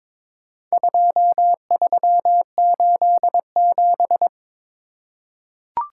Morse code, heard as a keyed tone, about 22 wpm, 700 Hz. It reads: 2387
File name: practice / 22wpm / Four Letter-Number Combinations - 22wpm